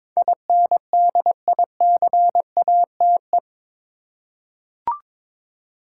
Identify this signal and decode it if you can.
Morse code, heard as a keyed tone, about 22 wpm, 700 Hz. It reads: INDICATE
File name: practice / 22wpm / Eight Letter Words - 22wpm